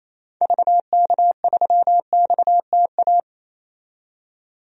Morse code, heard as a keyed tone, about 28 wpm, 700 Hz. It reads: VK3XTA
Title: DXCC-100 Call Signs - 6 Characters - 28wpm